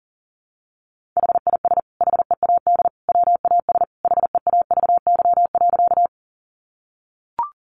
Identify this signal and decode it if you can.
Morse code, heard as a keyed tone, about 40 wpm, 700 Hz. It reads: HIS HEAD WAS HEAVY.
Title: Sentences from Top 1100 Words 40wpm